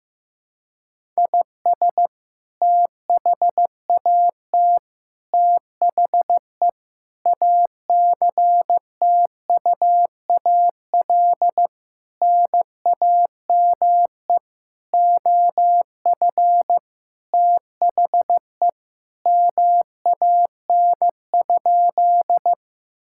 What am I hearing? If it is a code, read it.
Morse code, heard as a keyed tone, about 15 wpm, 700 Hz. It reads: IS THAT THE ACTUAL NAME OF THE MAN?
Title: Sentences from Top 1000 Words - Review Part 5 - 15wpm